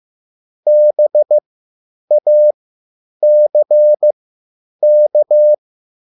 Morse code, heard as a keyed tone, about 15 wpm, spaced as for 9 wpm, 600 Hz. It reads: BACK
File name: QSL / CW lesson